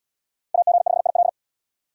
Morse code, heard as a keyed tone, about 77 wpm, 700 Hz. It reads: AG5IL